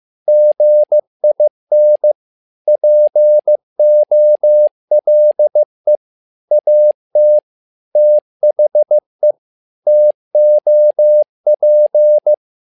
Morse code, heard as a keyed tone, about 15 wpm, 600 Hz. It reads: GIN POLE AT THE TOP
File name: learning center / tower trouble / tower trouble 15wpm